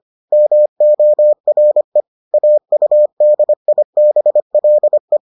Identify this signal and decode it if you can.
Morse code, heard as a keyed tone, about 25 wpm, 600 Hz. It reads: MORE AUDIBLE